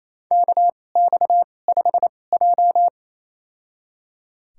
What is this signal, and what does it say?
Morse code, heard as a keyed tone, about 29 wpm, spaced as for 21 wpm, 690 Hz. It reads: KX5J
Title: USA Call Signs - 4 Characters - 2x Character Spacing - 28wpm